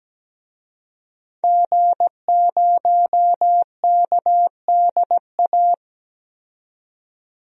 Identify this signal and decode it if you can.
Morse code, heard as a keyed tone, about 17 wpm, 700 Hz. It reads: G0KDA
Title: DXCC-100 Call Signs - 5 Characters - 17wpm